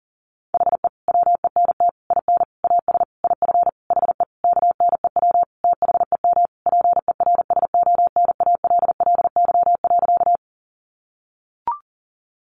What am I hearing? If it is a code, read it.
Morse code, heard as a keyed tone, about 40 wpm, 700 Hz. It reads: HE WENT IN AS IF HE KNEW THEM PERSONALLY.
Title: Sentences from Top 1400 Words 40wpm